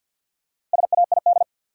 Morse code, heard as a keyed tone, about 54 wpm, 700 Hz. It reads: SAID
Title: Sentences from Top 100 Words - New Word Drill - 50wpm